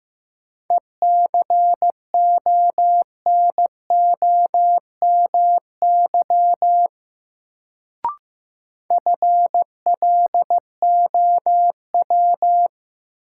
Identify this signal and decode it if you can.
Morse code, heard as a keyed tone, about 15 wpm, 700 Hz. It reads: ECONOMY FLOW
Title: Sentences from Top 1200 Words - New Word Drill Alt - 15wpm